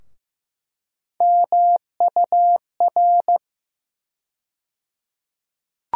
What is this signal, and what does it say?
Morse code, heard as a keyed tone, about 15 wpm, 700 Hz. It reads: MUR
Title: USA Call Sign Postfixes - 3 Letters 15wpm